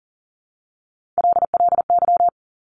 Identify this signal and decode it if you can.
Morse code, heard as a keyed tone, about 42 wpm, 690 Hz. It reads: LLY
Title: Top Three Letter Combinations in Words 40wpm